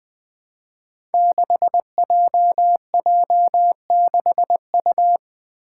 Morse code, heard as a keyed tone, about 20 wpm, 700 Hz. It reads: 6JJ6U
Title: Five Challenging Letter-Number Combinations 20wpm